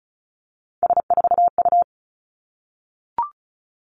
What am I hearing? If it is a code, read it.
Morse code, heard as a keyed tone, about 36 wpm, 700 Hz. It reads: S4U
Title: Three Challenging Letter-Number Combinations 35wpm